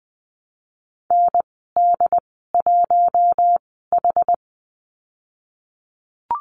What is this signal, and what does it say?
Morse code, heard as a keyed tone, about 20 wpm, spaced as for 15 wpm, 700 Hz. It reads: ND1H
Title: 069 Lesson 12g - Letter d - Call Signs 20wpm